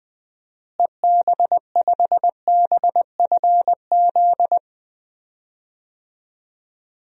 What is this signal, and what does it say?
Morse code, heard as a keyed tone, about 20 wpm, 700 Hz. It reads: EB5BFZ
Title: DXCC-100 Call Signs - 6 Characters - Repeated 2x - 20wpm